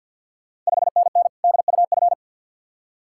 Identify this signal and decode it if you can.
Morse code, heard as a keyed tone, about 52 wpm, 690 Hz. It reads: 5NN DUF
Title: Ontario QSO Party Exchange 50wpm